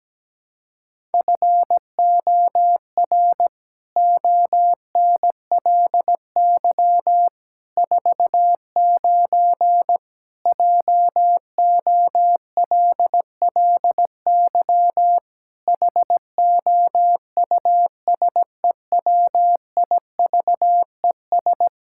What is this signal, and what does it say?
Morse code, heard as a keyed tone, about 17 wpm, 700 Hz. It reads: FOR ONLY 49 JOLLY HOUSEWIVES
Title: Pangram Challenge v2 - 17wpm